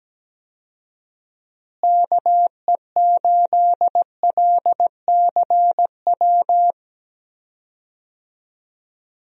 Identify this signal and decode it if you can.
Morse code, heard as a keyed tone, about 17 wpm, 700 Hz. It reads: KE8LCW
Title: USA Call Signs - 6 Characters - Repeated 2x - 17wpm